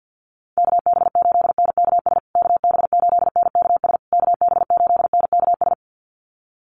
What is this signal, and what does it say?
Morse code, heard as a keyed tone, about 48 wpm, 700 Hz. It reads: KB8NKS KB8NKS KB8NKS
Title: USA Call Signs Repeated 3x 50wpm